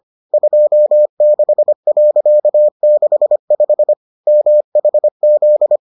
Morse code, heard as a keyed tone, about 25 wpm, 600 Hz. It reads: 26.65 MHZ